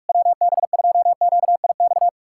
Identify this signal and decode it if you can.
Morse code, heard as a keyed tone, about 59 wpm, spaced as for 52 wpm, 700 Hz. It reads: WB2QIX
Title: Speed-Racing Common QSO Phrases - 45wpm